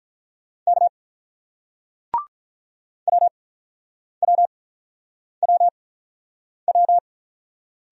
Morse code, heard as a keyed tone, about 47 wpm, spaced as for 9 wpm, 700 Hz. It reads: K WWWW